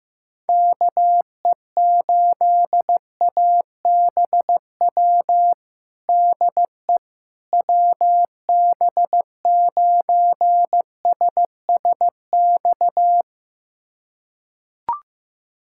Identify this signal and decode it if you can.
Morse code, heard as a keyed tone, about 15 wpm, 700 Hz. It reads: KE8ABW DE WB9SSX
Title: USA Callsign DE Callsign - 6 Characters - 15wpm